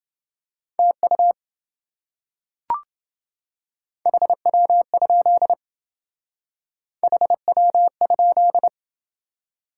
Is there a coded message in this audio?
Yes: Morse code from 0.8 to 8.7 s, heard as a keyed tone, about 28 wpm, 700 Hz. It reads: TU HW? HW?